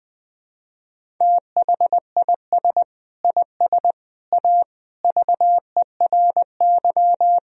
Morse code, heard as a keyed tone, about 21 wpm, 700 Hz. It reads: THIS IS A VERY
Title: Sentences from Top 1000 Words - Review Part 1 - 20wpm